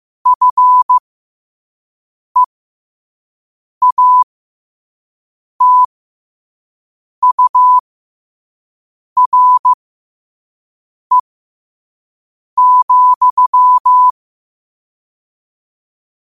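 Morse code, heard as a keyed tone, about 15 wpm, spaced as for 5 wpm, 1000 Hz. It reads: FEATURE,
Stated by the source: Morse sent at 5 wpm.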